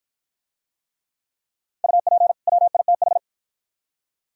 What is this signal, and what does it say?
Morse code, heard as a keyed tone, about 51 wpm, 700 Hz. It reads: UP WITH